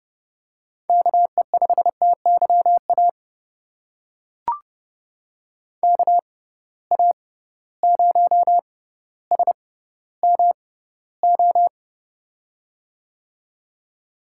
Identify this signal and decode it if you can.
Morse code, heard as a keyed tone, about 32 wpm, spaced as for 16 wpm, 700 Hz. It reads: KE5TYA KA0SMO
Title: USA Call Signs - 6 Characters - 6x Character Spacing - 30wpm